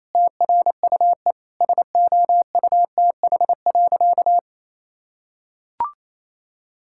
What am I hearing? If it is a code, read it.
Morse code, heard as a keyed tone, about 28 wpm, 700 Hz. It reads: TRUE SOUTH.
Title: Sentences from Top 1000 Words - Review Part 6 - 28wpm